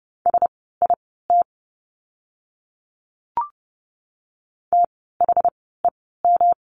Morse code, heard as a keyed tone, about 30 wpm, spaced as for 17 wpm, 700 Hz. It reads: SIT THEM
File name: course / 30:10wpm / 115 Lesson 19f - Letter w - Words 30wpm